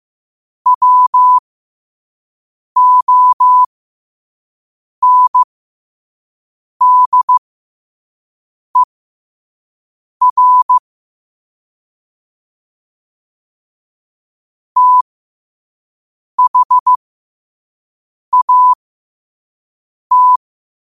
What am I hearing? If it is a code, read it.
Morse code, heard as a keyed tone, about 15 wpm, spaced as for 5 wpm, 1000 Hz. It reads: WONDER THAT
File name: qotdfiles / 5wpm